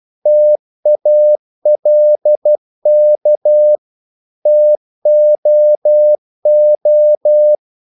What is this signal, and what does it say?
Morse code, heard as a keyed tone, about 12 wpm, 600 Hz. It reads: TALK TOO
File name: CW / Practice / If12